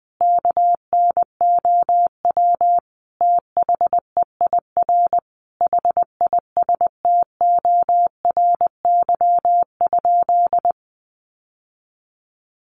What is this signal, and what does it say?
Morse code, heard as a keyed tone, about 20 wpm, 700 Hz. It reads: KNOW THEIR HISTORY?
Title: Sentences from Top 600 Words 20wpm